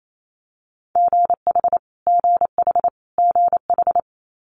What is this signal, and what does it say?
Morse code, heard as a keyed tone, about 28 wpm, 700 Hz. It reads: GH GH GH
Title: Multi-letter Phonemes Repeated 3x 28wpm